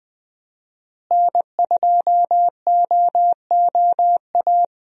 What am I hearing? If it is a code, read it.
Morse code, heard as a keyed tone, about 20 wpm, 700 Hz. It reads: N2OOA
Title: USA Call Signs - 5 Characters - Repeated 2x - 20wpm